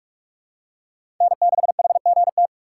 Morse code, heard as a keyed tone, about 46 wpm, 700 Hz. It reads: N6HGT
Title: USA Call Signs - 5 Characters - 4x Character Spacing - 45wpm